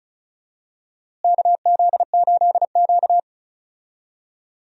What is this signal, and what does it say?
Morse code, heard as a keyed tone, about 34 wpm, 700 Hz. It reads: KZ8Q
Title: USA Call Signs - 4 Characters - Repeated 2x - 35wpm